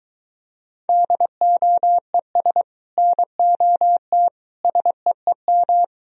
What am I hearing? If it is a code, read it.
Morse code, heard as a keyed tone, about 23 wpm, 700 Hz. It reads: DOES NOT SEEM